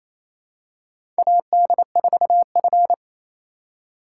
Morse code, heard as a keyed tone, about 27 wpm, 700 Hz. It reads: AD4F
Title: USA Call Signs - 3x Character Spacing - 28wpm